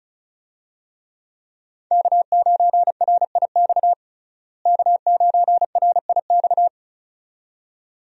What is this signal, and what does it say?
Morse code, heard as a keyed tone, about 36 wpm, 700 Hz. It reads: K9RIX K9RIX
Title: USA Call Signs - 5 Characters - Repeated 2x - 35wpm